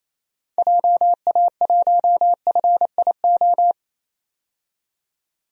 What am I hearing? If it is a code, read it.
Morse code, heard as a keyed tone, about 28 wpm, 700 Hz. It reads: JA1FIO
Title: DXCC-100 Call Signs - 6 Characters - 28wpm